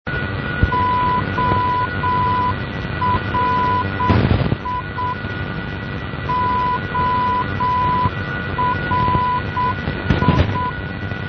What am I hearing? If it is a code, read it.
Morse code, heard as a keyed tone, about 7 wpm, 1000 Hz, with noise. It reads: ORI ORI
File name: ORI NDB 1988